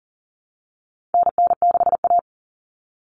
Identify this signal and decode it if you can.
Morse code, heard as a keyed tone, about 40 wpm, 700 Hz. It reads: NN6A
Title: Common Cut Numbers - 4-Digit 40wpm